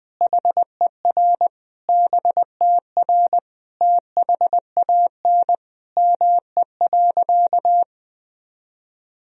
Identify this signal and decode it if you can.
Morse code, heard as a keyed tone, about 20 wpm, 700 Hz. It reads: HER BTR THAN ME.
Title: Sentences from Top 400 Words and Abbreviations 20wpm